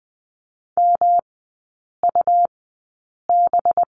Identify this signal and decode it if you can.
Morse code, heard as a keyed tone, about 20 wpm, spaced as for 8 wpm, 700 Hz. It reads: MUB
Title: 184 Lesson 30e - Letter b - Three Characters 20wpm